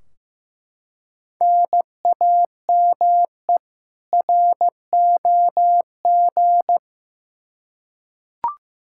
Morse code, heard as a keyed tone, about 15 wpm, 700 Hz. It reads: NAME ROG
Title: Practice QSO - Exchange 1 - Beginning - 15wpm